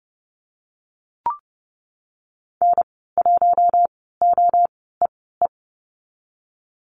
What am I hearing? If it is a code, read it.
Morse code, heard as a keyed tone, about 30 wpm, spaced as for 17 wpm, 700 Hz. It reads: N1OEE
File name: Lesson 19g - Letter w - Call Signs 30wpm